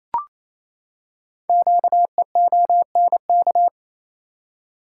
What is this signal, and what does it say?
Morse code, heard as a keyed tone, about 28 wpm, 700 Hz. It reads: QEONK
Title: Code Groups - 5 Letters - 28wpm